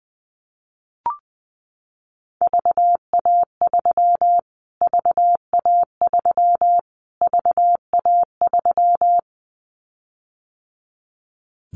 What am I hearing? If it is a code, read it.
Morse code, heard as a keyed tone, about 20 wpm, 700 Hz. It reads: VA3 VA3 VA3
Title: Callsign Prefixes for Top 25 DXCC - Repeated 3x - 20wpm